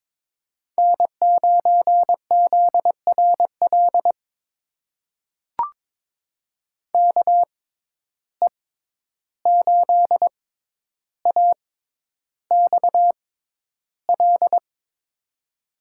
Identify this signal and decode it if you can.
Morse code, heard as a keyed tone, about 23 wpm, spaced as for 11 wpm, 700 Hz. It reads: N9ZRL KE8AXL
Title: USA Call Signs - 6x Character Spacing - 22wpm